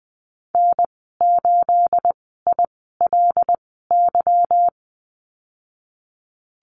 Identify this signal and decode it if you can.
Morse code, heard as a keyed tone, about 19 wpm, spaced as for 14 wpm, 700 Hz. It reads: N8ILY
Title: 213 Lesson 34g - Number 8 - Call Signs 20wpm